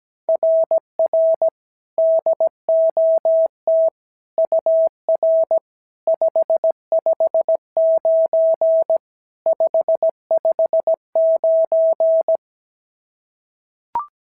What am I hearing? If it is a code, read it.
Morse code, heard as a keyed tone, about 16 wpm, 650 Hz. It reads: RR DOT UR 559 559
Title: SOTA QSO Practice 17wpm